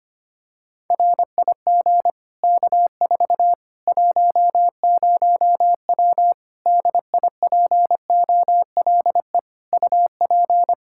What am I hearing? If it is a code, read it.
Morse code, heard as a keyed tone, about 25 wpm, 700 Hz. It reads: RIG K4 10W DIPOLE UP
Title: Practice QSO - Exchange 2 - Info - 25wpm